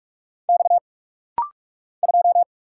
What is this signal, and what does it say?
Morse code, heard as a keyed tone, about 45 wpm, 700 Hz. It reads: X 2
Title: Single Letter-Number Alt - Rapid-Fire 45wpm